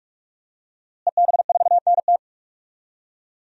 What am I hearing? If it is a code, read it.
Morse code, heard as a keyed tone, about 46 wpm, 700 Hz. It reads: EB4NT
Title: DXCC-100 Call Signs - 5 Characters - Repeated 2x - 45wpm